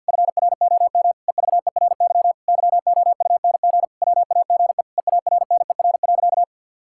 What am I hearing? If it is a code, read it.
Morse code, heard as a keyed tone, about 46 wpm, 700 Hz. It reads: FROM EVERY YOUNG WAGE EARNER.
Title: Pangram Challenge v2 - 50wpm